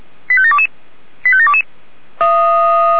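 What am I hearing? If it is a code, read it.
Five-tone selective call (ZVEI), heard as quick runs of five tones, tones of about 70 ms. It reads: 85722 85722
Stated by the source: Five-tone ZVEI selective call 85753 received by SDR.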